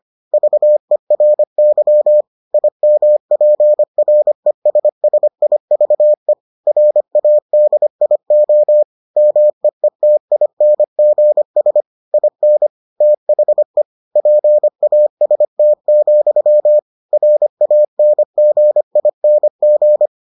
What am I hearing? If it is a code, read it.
Morse code, heard as a keyed tone, about 25 wpm, 600 Hz. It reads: VERY IMPRESSIVE RADIO MEETINGS IN THE PAST, RANGING